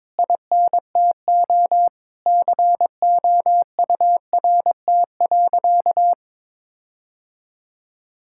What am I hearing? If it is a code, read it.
Morse code, heard as a keyed tone, about 22 wpm, 700 Hz. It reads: INTO COURT.
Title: Sentences from Top 1600 Words 22wpm